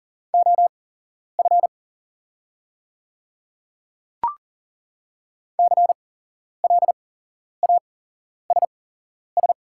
Morse code, heard as a keyed tone, about 41 wpm, spaced as for 11 wpm, 700 Hz. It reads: OF CLASS
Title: 8x Character Spacing - Top 500 Words - 40wpm